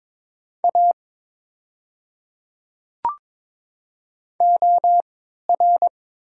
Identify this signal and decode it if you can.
Morse code, heard as a keyed tone, about 22 wpm, spaced as for 13 wpm, 700 Hz. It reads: A OR
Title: 3x Character Spacing - Top 200 Words - 22wpm